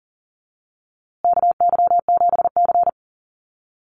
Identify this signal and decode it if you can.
Morse code, heard as a keyed tone, about 40 wpm, 700 Hz. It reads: KY7C
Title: USA Call Signs Repeated 3x 40wpm